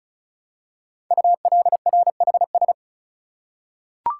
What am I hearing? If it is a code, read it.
Morse code, heard as a keyed tone, about 36 wpm, 700 Hz. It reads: ULRHS
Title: Code Groups - 5 Letters - 35wpm